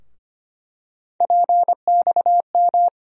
Morse code, heard as a keyed tone, about 25 wpm, 700 Hz. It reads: PXM
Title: USA Call Sign Postfixes - 3 Letters 25wpm